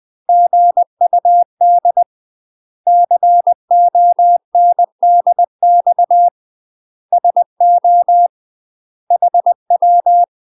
Morse code, heard as a keyed tone, about 20 wpm, 700 Hz. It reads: GUD CONDX SO HW
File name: UK MASTER QSOs TRAINING / 20WPM / UK MASTER QSOs 0002